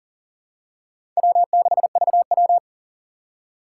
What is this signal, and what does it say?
Morse code, heard as a keyed tone, about 38 wpm, 700 Hz. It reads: W6VW
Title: USA Call Signs - 4x Character Spacing - 40wpm